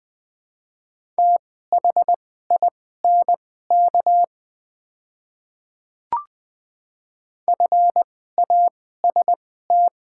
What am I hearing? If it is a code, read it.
Morse code, heard as a keyed tone, about 21 wpm, spaced as for 15 wpm, 700 Hz. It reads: THINK FAST
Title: 2x Character Spacing - Top 500 Words - 20wpm